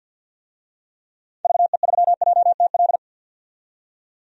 Morse code, heard as a keyed tone, about 52 wpm, 700 Hz. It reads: VE3JTL
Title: DXCC-100 Call Signs - 6 Characters - 50wpm